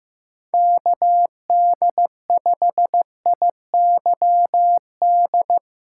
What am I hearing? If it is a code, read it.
Morse code, heard as a keyed tone, about 15 wpm, 700 Hz. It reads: KD5IYD
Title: USA Call Signs - 6 Characters - 8x Character Spacing - 15wpm